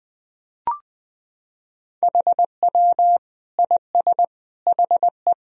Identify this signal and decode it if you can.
Morse code, heard as a keyed tone, about 20 wpm, 700 Hz. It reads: HW IS HE
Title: Sentences from Top 100 Words and Abbreviations 20wpm